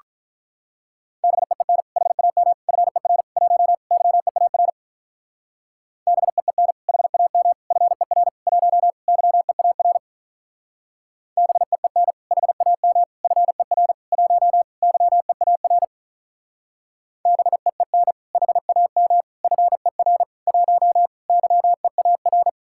Morse code, heard as a keyed tone, about 44 wpm, 700 Hz. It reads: BEEN HAM FER 1 YEAR BEEN HAM FER 1 YEAR BEEN HAM FER 1 YEAR BEEN HAM FER 1 YEAR